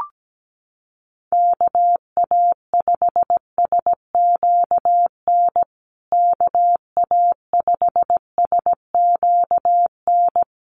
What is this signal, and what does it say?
Morse code, heard as a keyed tone, about 17 wpm, 700 Hz. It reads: KA5SQN KA5SQN
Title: USA Call Signs Repeated 3x 17wpm